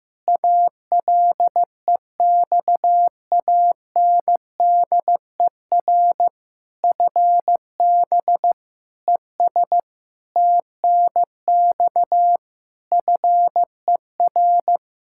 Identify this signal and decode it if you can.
Morse code, heard as a keyed tone, about 15 wpm, 700 Hz. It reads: ALEXANDER FB ES TNX FER
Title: Practice QSO - Exchange 2 - Info - 15wpm